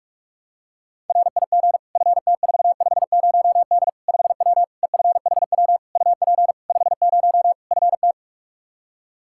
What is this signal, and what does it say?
Morse code, heard as a keyed tone, about 46 wpm, 700 Hz. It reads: RIG FT450D 5W EFHW UP 50 FT
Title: Practice QSO - Exchange 2 - Info - 45wpm